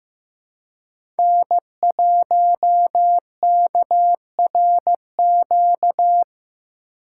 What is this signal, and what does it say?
Morse code, heard as a keyed tone, about 15 wpm, 700 Hz. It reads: N1KRQ
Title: USA Call Signs - 5 Characters - 4x Character Spacing - 15wpm